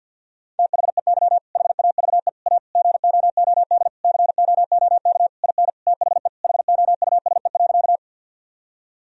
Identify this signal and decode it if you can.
Morse code, heard as a keyed tone, about 50 wpm, 700 Hz. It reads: THEY HAVE A GOOD COOK IN THE HOUSE.
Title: Sentences from Top 1000 Words - Review Part 4 - 50wpm